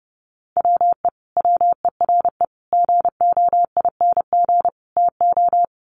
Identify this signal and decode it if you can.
Morse code, heard as a keyed tone, about 30 wpm, 700 Hz. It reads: WE WERE GOING TO
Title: Sets of 4 Words 30wpm